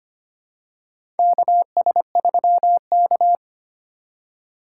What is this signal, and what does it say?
Morse code, heard as a keyed tone, about 25 wpm, 700 Hz. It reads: KS3K
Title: USA Call Signs - 4 Characters - 4x Character Spacing - 25wpm